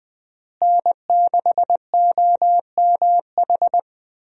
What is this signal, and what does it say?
Morse code, heard as a keyed tone, about 20 wpm, 700 Hz. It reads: N6OMH
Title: USA Call Signs - 5 Characters - Repeated 3x - 20wpm